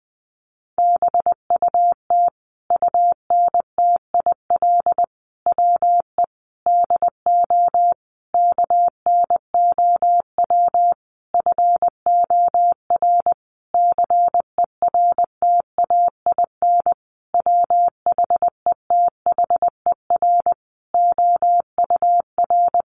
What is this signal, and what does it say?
Morse code, heard as a keyed tone, about 19 wpm, 700 Hz. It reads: BUT UNTIL WE DO KNOW FOR CERTAIN WHETHER OUR
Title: A Sound Of Thunder 20wpm